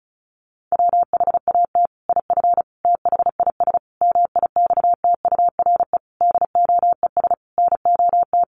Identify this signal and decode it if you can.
Morse code, heard as a keyed tone, about 34 wpm, 700 Hz. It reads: WHAT IF THIS MIXTURE DOES NOT
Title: Sentences from Top 1500 Words 35wpm